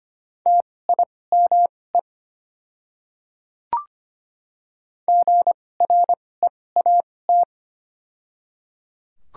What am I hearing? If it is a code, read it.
Morse code, heard as a keyed tone, about 25 wpm, spaced as for 18 wpm, 700 Hz. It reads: TIME GREAT
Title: 2x Character Spacing - Top 200 Words - 25wpm